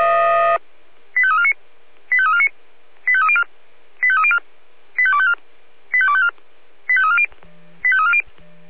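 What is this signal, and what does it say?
Five-tone selective call (ZVEI), heard as quick runs of five tones, tones of about 70 ms. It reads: 85349 85349 85304 85304 85254 85254 85340 85340
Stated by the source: ZVEI five-tone call received on SDR.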